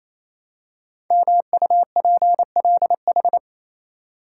Morse code, heard as a keyed tone, about 29 wpm, 700 Hz. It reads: MUPLH
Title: Code Groups - 5 Letters - 28wpm